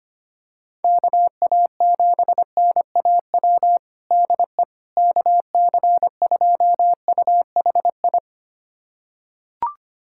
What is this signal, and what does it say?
Morse code, heard as a keyed tone, about 25 wpm, 700 Hz. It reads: KA7NAW DE KC2UHI
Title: USA Callsign DE Callsign 25wpm